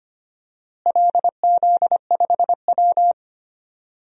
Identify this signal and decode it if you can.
Morse code, heard as a keyed tone, about 25 wpm, 700 Hz. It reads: LZ5W
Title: DXCC-100 Call Signs - 4 Characters - Repeated 3x - 25wpm